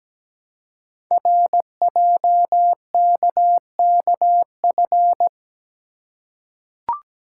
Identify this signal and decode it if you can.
Morse code, heard as a keyed tone, about 17 wpm, 700 Hz. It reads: RJKKF
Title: Code Groups - 5 Letters - 17wpm